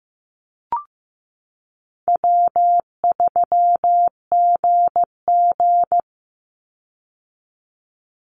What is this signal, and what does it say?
Morse code, heard as a keyed tone, about 15 wpm, 700 Hz. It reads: W3GG
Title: Easy Call Signs Repeated 3x - 15wpm